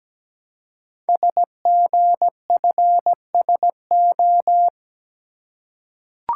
Morse code, heard as a keyed tone, about 17 wpm, 700 Hz. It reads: SGFSO
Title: Code Groups - 5 Letters - 17wpm